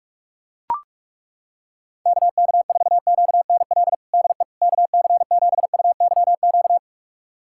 Morse code, heard as a keyed tone, about 43 wpm, 700 Hz. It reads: KK4QNL DE KC7UYQ
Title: USA Callsign DE Callsign - 6 Characters - 45wpm